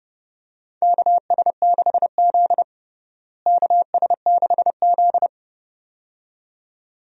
Morse code, heard as a keyed tone, about 32 wpm, 700 Hz. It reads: KS6Z KS6Z
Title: USA Call Signs - 4 Characters - Repeated 2x - 30wpm